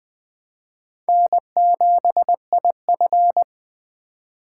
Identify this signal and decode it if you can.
Morse code, heard as a keyed tone, about 23 wpm, 700 Hz. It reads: N7IF
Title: Speed Racing - USA 1x2 Call Signs 20wpm